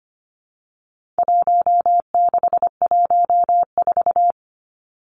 Morse code, heard as a keyed tone, about 25 wpm, 700 Hz. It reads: 1614
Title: Numbers - Contest Style - 4-Digit 25wpm